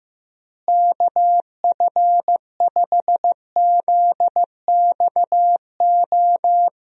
Morse code, heard as a keyed tone, about 15 wpm, 700 Hz. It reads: KF5ZXO
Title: USA Call Signs - 6 Characters - Repeated 3x - 15wpm